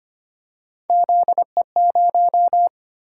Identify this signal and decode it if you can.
Morse code, heard as a keyed tone, about 25 wpm, 700 Hz. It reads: ZE0
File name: Three Letter-Number Combinations - 25wpm